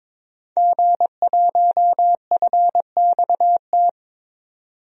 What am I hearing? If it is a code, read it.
Morse code, heard as a keyed tone, about 22 wpm, 700 Hz. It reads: G1FXT
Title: DXCC-100 Call Signs - 5 Characters - 22wpm